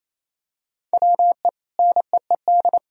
Morse code, heard as a keyed tone, about 28 wpm, 700 Hz. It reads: WE NEED
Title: Sentences from Top 1000 Words - Review Part 4 - 28wpm